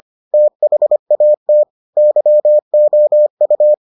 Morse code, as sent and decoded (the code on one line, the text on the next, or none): - .... .- - / -.-- --- ..-
THAT YOU